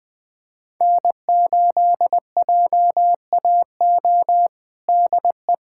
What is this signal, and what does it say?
Morse code, heard as a keyed tone, about 20 wpm, 700 Hz. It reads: N8JAO DE
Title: USA Callsign DE Callsign 20wpm